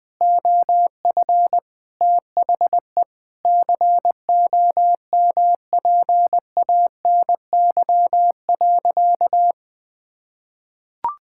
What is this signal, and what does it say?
Morse code, heard as a keyed tone, about 21 wpm, 700 Hz. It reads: OF THE COMPANY.
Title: Sentences with 3-Word Phrases - 20wpm